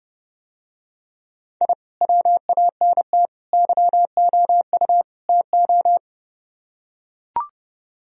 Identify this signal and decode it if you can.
Morse code, heard as a keyed tone, about 30 wpm, 700 Hz. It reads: I WANT YOU TO